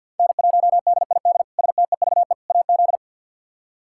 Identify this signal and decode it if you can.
Morse code, heard as a keyed tone, about 51 wpm, 700 Hz. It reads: N1DID STEVE AZ
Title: K1USN SST QSO Practice - 50wpm